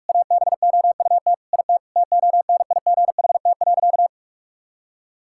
Morse code, heard as a keyed tone, about 46 wpm, 690 Hz. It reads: ABOUT IT TONIGHT.
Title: Sentences from Top 1000 Words - Review Part 5 - 45wpm